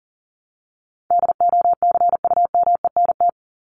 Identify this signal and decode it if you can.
Morse code, heard as a keyed tone, about 38 wpm, 700 Hz. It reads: DOCUMENT
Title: Top 1751-2000 Words 40wpm